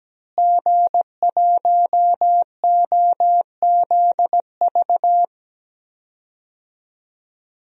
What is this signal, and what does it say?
Morse code, heard as a keyed tone, about 17 wpm, 700 Hz. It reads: G1OZV
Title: DXCC-100 Call Signs - 5 Characters - Repeated 2x - 17wpm